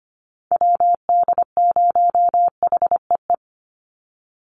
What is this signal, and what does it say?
Morse code, heard as a keyed tone, about 25 wpm, 700 Hz. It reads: WD0HEE
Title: USA Call Signs 25wpm